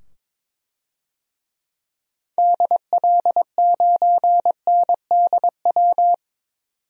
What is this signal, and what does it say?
Morse code, heard as a keyed tone, about 22 wpm, 700 Hz. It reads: DL9NDW
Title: DXCC-100 Call Signs - 6 Characters - 22wpm